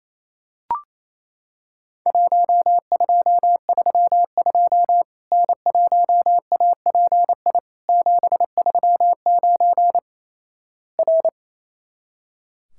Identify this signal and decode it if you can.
Morse code, heard as a keyed tone, about 28 wpm, 700 Hz. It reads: 1232 N1API 739 R
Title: Worked All Europe DX Contest - QTC Exchange - 28wpm